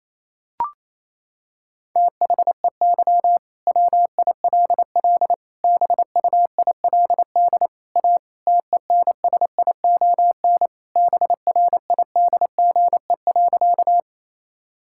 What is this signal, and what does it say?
Morse code, heard as a keyed tone, about 28 wpm, 700 Hz. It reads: THEY WILL BUILD A TENSION BRIDGE.
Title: Sentences from Top 1600 Words 28wpm